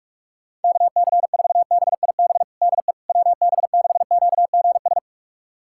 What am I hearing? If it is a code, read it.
Morse code, heard as a keyed tone, about 43 wpm, 700 Hz. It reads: KC4BIB DE WB6QGS
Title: USA Callsign DE Callsign - 6 Characters - 45wpm